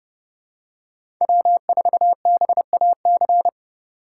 Morse code, heard as a keyed tone, about 29 wpm, 700 Hz. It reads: W4BAC